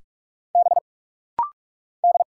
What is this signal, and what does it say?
Morse code, heard as a keyed tone, about 46 wpm, 700 Hz. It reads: B D
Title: Single Letter-Number Alt - Rapid-Fire 45wpm